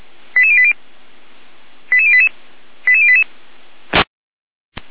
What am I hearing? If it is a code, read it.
Five-tone selective call (ZVEI), heard as quick runs of five tones, tones of about 70 ms. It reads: 88080 88088 88088